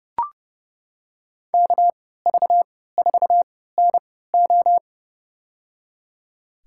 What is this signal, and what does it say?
Morse code, heard as a keyed tone, about 32 wpm, spaced as for 17 wpm, 700 Hz. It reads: KV4NO
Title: USA Call Signs - 5 Characters - 3x Character Spacing - 30wpm